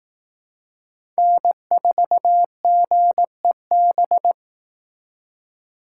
Morse code, heard as a keyed tone, about 18 wpm, 700 Hz. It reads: N4GEB
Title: Speed Racing - USA Call Signs 15wpm